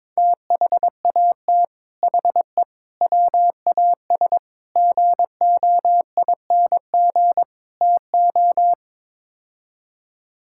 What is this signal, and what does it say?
Morse code, heard as a keyed tone, about 22 wpm, 700 Hz. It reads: THAT HE WAS GOING TO
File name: practice / 22wpm / Build A Phrase - 5-Word Phrases - 22wpm